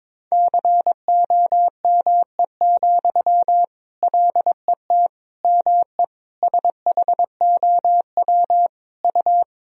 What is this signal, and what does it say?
Morse code, heard as a keyed tone, about 22 wpm, 700 Hz. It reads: COME, LET ME SHOW U
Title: Sentences from Top 500 Words and Abbreviations 22wpm